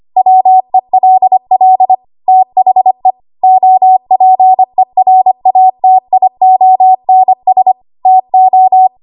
Morse code, heard as a keyed tone, about 25 wpm, 750 Hz. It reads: WELL THE OPERATIONS TO
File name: Archive / 25 WPM / 25WPM